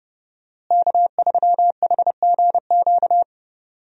Morse code, heard as a keyed tone, about 30 wpm, 700 Hz. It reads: K3HGQ
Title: USA Call Signs - 5 Characters - Repeated 3x - 30wpm